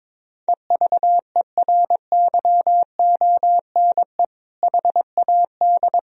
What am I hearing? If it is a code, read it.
Morse code, heard as a keyed tone, about 21 wpm, 700 Hz. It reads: EVERYONE HAD
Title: Sentences from Top 1000 Words - Review Part 2 - 22wpm